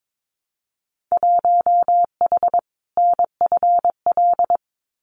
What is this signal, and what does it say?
Morse code, heard as a keyed tone, about 22 wpm, 700 Hz. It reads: 1H NFL
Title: Winter Field Day Exchanges 22wpm